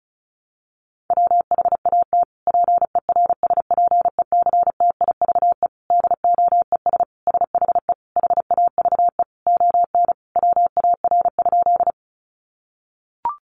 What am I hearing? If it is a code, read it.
Morse code, heard as a keyed tone, about 34 wpm, 700 Hz. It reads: WHAT PERSPECTIVE DOES SHE HAVE ON WAR?
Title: Sentences from Top 1200 Words 35wpm